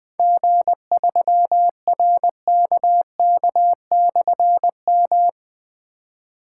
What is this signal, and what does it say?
Morse code, heard as a keyed tone, about 20 wpm, 700 Hz. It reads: G3RKK/M
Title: Difficult Call Signs 20wpm